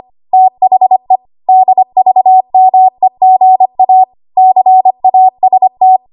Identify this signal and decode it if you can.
Morse code, heard as a keyed tone, about 25 wpm, 750 Hz. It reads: THE DVMEGA CAST
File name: Archive / 25 WPM / 25WPM